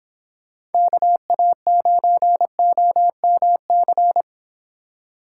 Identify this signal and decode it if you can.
Morse code, heard as a keyed tone, about 28 wpm, 700 Hz. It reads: KA9OMC